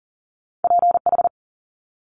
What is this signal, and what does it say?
Morse code, heard as a keyed tone, about 39 wpm, 700 Hz. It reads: PH
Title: Multi-letter Phonemes Quick Fire 40wpm